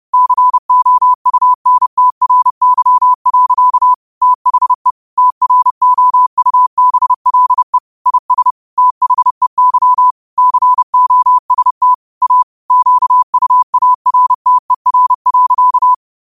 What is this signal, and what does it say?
Morse code, heard as a keyed tone, about 30 wpm, 1000 Hz. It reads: COUNTRY. THE TROUBLE IS THEY COST A QUARTER.
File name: qotdfiles / 30wpm